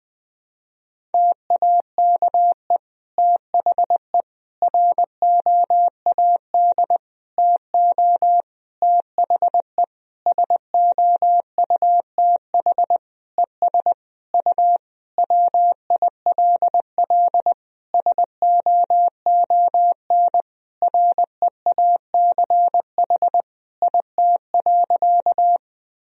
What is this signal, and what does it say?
Morse code, heard as a keyed tone, about 20 wpm, 700 Hz. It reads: TAKE THE ROAD TO THE SOUTH ES U WILL SOON REACH IT.
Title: Sentences from Top 400 Words and Abbreviations 20wpm